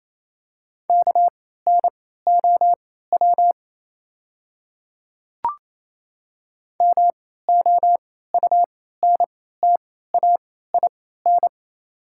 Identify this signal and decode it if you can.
Morse code, heard as a keyed tone, about 28 wpm, spaced as for 16 wpm, 700 Hz. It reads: KNOW MOUNTAIN
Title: 3x Character Spacing - Top 300 Words - 28wpm